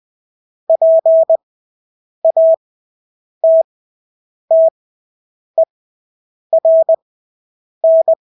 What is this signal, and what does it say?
Morse code, heard as a keyed tone, about 20 wpm, spaced as for 8 wpm, 650 Hz. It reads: PATTERN
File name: Engelske ord 20/8 wpm